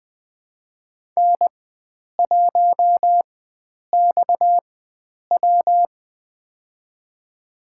Morse code, heard as a keyed tone, about 20 wpm, spaced as for 9 wpm, 700 Hz. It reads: N1XW
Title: USA Call Signs - 4 Characters - 4x Character Spacing - 20wpm